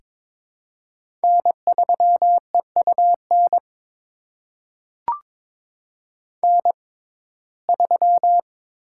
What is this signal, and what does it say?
Morse code, heard as a keyed tone, about 22 wpm, spaced as for 16 wpm, 700 Hz. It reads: N3EUN N3
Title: USA Call Signs - 5 Characters - 6x Character Spacing - 22wpm